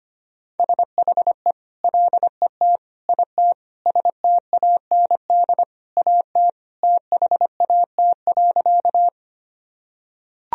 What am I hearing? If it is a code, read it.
Morse code, heard as a keyed tone, about 25 wpm, 700 Hz. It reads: SHE LET IT STAND AT THAT.
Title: Sentences from Top 1000 Words - Review Part 1 - 25wpm